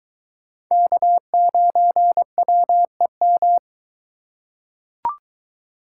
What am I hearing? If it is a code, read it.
Morse code, heard as a keyed tone, about 23 wpm, 700 Hz. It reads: K9WEM